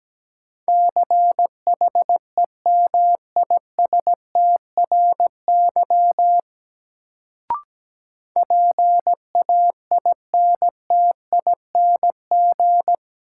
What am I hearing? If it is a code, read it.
Morse code, heard as a keyed tone, about 17 wpm, 700 Hz. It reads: CHEMISTRY PAINTING
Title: Sentences from Top 1000 Words - Review Part 6 - New Word Drill - 17wpm